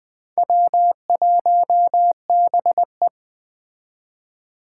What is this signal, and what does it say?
Morse code, heard as a keyed tone, about 21 wpm, 700 Hz. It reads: W1BE